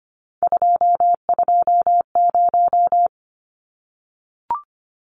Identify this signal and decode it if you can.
Morse code, heard as a keyed tone, about 25 wpm, 700 Hz. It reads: 220
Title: Numbers - 3-Digit 25wpm